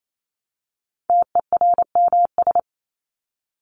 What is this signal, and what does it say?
Morse code, heard as a keyed tone, about 28 wpm, 700 Hz. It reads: TERMS
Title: ES:S Suffix - Top 100 Words with Base Words 28wpm